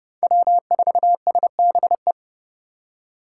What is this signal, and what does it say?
Morse code, heard as a keyed tone, about 30 wpm, 700 Hz. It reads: W4SBE
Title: USA Call Signs - 6x Character Spacing - 30wpm